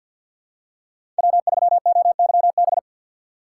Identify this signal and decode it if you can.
Morse code, heard as a keyed tone, about 52 wpm, 700 Hz. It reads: W3OYB
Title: USA Call Signs - 5 Characters - 4x Character Spacing - 50wpm